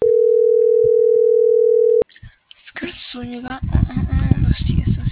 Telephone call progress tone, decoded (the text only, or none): ringback tone